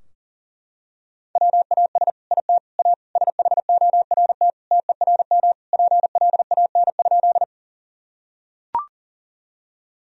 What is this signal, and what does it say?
Morse code, heard as a keyed tone, about 40 wpm, 700 Hz. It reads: WAS IT A SHORT TERM PLAN?
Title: Sentences from Top 1000 Words - Review Part 4 - 40wpm